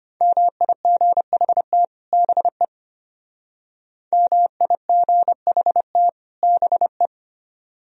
Morse code, heard as a keyed tone, about 32 wpm, 700 Hz. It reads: MIGHT BE MIGHT BE